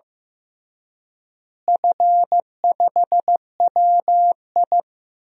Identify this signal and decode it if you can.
Morse code, heard as a keyed tone, about 15 wpm, 700 Hz. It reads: F5WI
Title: DXCC-100 Call Signs - 4 Characters - Repeated 3x - 15wpm